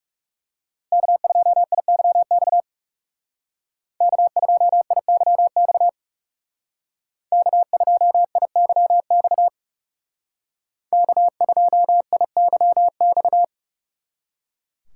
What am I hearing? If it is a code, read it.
Morse code, heard as a keyed tone, about 39 wpm, 700 Hz. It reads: K2IYX K2IYX K2IYX K2IYX